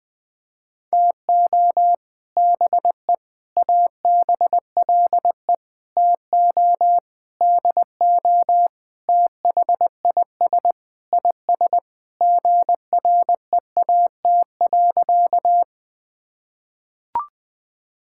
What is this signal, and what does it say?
Morse code, heard as a keyed tone, about 20 wpm, 700 Hz. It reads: TO BE ABLE TO DO THIS IS GREAT.
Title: Sentences with 5-Word Phrases - Spoken Beginning - 20wpm